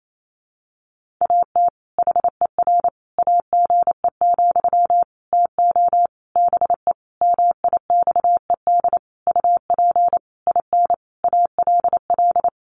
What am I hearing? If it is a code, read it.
Morse code, heard as a keyed tone, about 28 wpm, 700 Hz. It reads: AT HER AGE, TO BE MIXED UP IN ALL
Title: Sentences from Top 1500 Words 28wpm